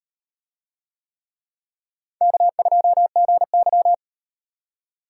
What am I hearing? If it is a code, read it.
Morse code, heard as a keyed tone, about 36 wpm, 700 Hz. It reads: K2GY